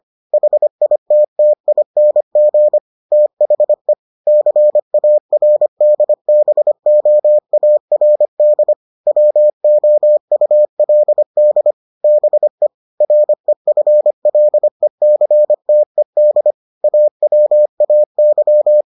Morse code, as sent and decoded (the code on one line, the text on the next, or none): .... .. - - .. -. --. / - .... / -.-. .- .-. -.. -... --- .- .-. -.. / .-- --- ..- .-.. -.. / -... / .-. ..-. .-.. -.-. - . -.. / .- .-- .- -.--
HITTING THE CARDBOARD WOULD BE REFLECTED AWAY